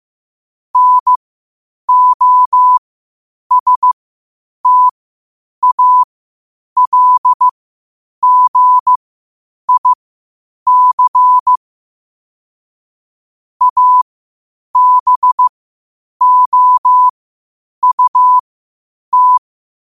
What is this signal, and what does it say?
Morse code, heard as a keyed tone, about 15 wpm, spaced as for 8 wpm, 1000 Hz. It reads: NOSTALGIC ABOUT
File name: qotdfiles / 8wpm